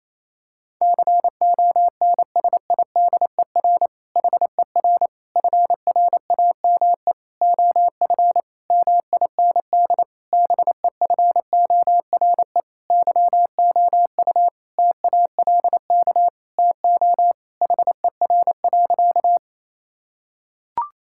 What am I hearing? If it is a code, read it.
Morse code, heard as a keyed tone, about 28 wpm, 700 Hz. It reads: CONSIDER HER FRAME OF MIND BEFORE YOU TALK TO HER.
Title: Sentences from Top 1000 Words - Review Part 6 - 28wpm